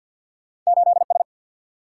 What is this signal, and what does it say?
Morse code, heard as a keyed tone, about 50 wpm, 700 Hz. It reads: 8 S
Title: Single-Letter Single-Number 50wpm